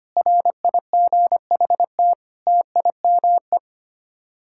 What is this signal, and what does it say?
Morse code, heard as a keyed tone, about 25 wpm, 700 Hz. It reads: RIGHT TIME
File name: Sets of 9 Words - 25wpm